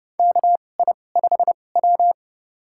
Morse code, heard as a keyed tone, about 31 wpm, spaced as for 22 wpm, 700 Hz. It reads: KI5W
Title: USA Call Signs - 4 Characters - 2x Character Spacing - 30wpm